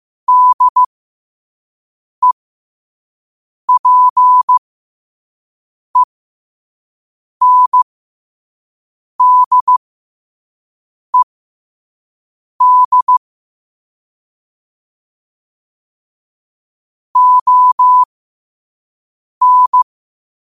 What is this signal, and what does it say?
Morse code, heard as a keyed tone, about 15 wpm, spaced as for 5 wpm, 1000 Hz. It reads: DEPENDED ON
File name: qotdfiles / 5wpm